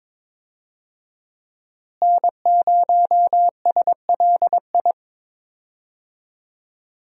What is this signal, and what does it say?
Morse code, heard as a keyed tone, about 22 wpm, 700 Hz. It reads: N0SLI
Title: USA Call Signs - 5 Characters - Repeated 2x - 22wpm